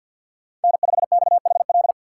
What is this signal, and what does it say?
Morse code, heard as a keyed tone, about 52 wpm, 690 Hz. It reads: N5XHL
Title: USA Call Signs - 5 Characters - 2x Character Spacing - 50wpm